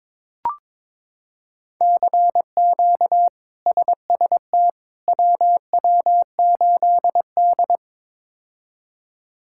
Morse code, heard as a keyed tone, about 22 wpm, 700 Hz. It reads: CQ SST WW8D
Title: K1USN SST QSO Practice - 22wpm